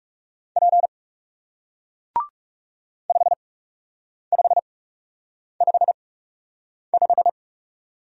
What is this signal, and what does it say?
Morse code, heard as a keyed tone, about 40 wpm, spaced as for 8 wpm, 700 Hz. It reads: P 5555